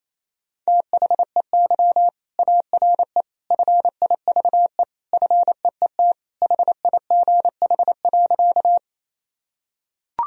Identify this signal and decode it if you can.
Morse code, heard as a keyed tone, about 28 wpm, 700 Hz. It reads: THEY ARE FIVE FEET HIGH.
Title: Sentences from Top 1000 Words - Review Part 2 - 28wpm